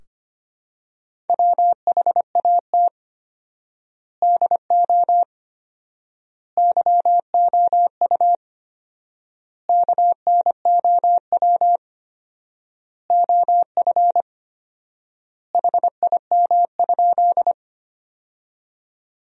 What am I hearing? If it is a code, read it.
Morse code, heard as a keyed tone, about 25 wpm, 700 Hz. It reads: WHAT DO YOU KNOW OF HIM?
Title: 4x Word-Spacing - Sentences from Top 100 Words - 25wpm